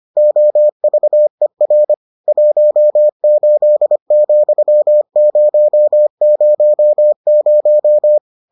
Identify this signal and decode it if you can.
Morse code, heard as a keyed tone, about 25 wpm, 600 Hz. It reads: OVER 18,000